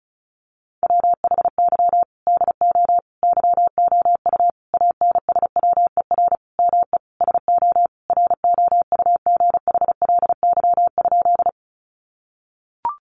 Morse code, heard as a keyed tone, about 34 wpm, 700 Hz. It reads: WHY DO YOU ANSWER ME SO ROUGHLY?
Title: Sentences from Top 1500 Words 35wpm